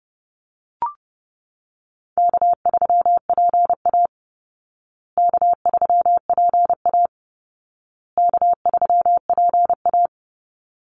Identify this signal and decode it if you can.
Morse code, heard as a keyed tone, about 30 wpm, 700 Hz. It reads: K3PA K3PA K3PA
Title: Easy Call Signs Repeated 3x - 30wpm